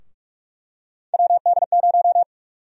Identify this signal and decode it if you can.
Morse code, heard as a keyed tone, about 46 wpm, 700 Hz. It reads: WD0
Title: USA Call Sign Prefixes - 45wpm